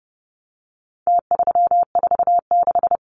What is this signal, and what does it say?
Morse code, heard as a keyed tone, about 30 wpm, 700 Hz. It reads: T346
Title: Numbers - Contest Style - 4-Digit 30wpm